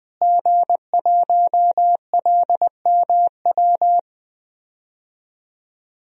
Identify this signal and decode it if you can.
Morse code, heard as a keyed tone, about 21 wpm, 700 Hz. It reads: G1LMW
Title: DXCC-100 Call Signs - 5 Characters - Repeated 2x - 20wpm